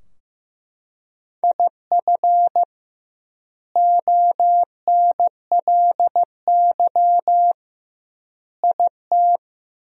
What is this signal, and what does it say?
Morse code, heard as a keyed tone, about 15 wpm, 700 Hz. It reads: IF ONLY IT
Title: 2x Word-Spacing - Sentences from Top 500 Words - 15wpm